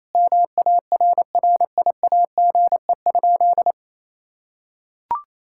Morse code, heard as a keyed tone, about 28 wpm, 700 Hz. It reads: MARRIAGE?
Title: Sentences from Top 1000 Words - Review Part 5 - 28wpm